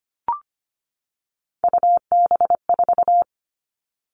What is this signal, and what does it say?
Morse code, heard as a keyed tone, about 25 wpm, 700 Hz. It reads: UB4
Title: Three Challenging Letter-Number Combinations 25wpm